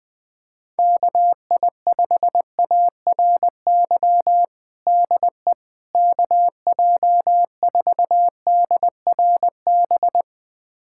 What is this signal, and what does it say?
Morse code, heard as a keyed tone, about 20 wpm, 700 Hz. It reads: KI5ARY DE KJ4DRB
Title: USA Callsign DE Callsign 20wpm